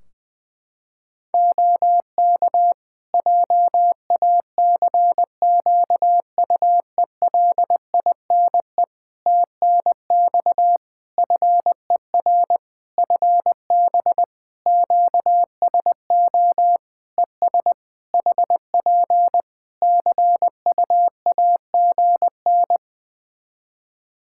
Morse code, heard as a keyed tone, about 20 wpm, 700 Hz. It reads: OK JACQUELINE TNX FER FB QSO ES HP CUAGN
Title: K1USN SST QSO Practice 20wpm